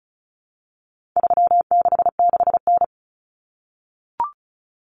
Morse code, heard as a keyed tone, about 34 wpm, 700 Hz. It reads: 366N
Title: Common Cut Numbers - 4-Digit 35wpm